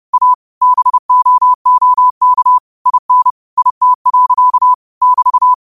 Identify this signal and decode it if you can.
Morse code, heard as a keyed tone, about 30 wpm, 1000 Hz. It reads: A BOOK IN IT. <BT>
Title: 30 WPM morse code quotes for Thu, 29 Feb 2024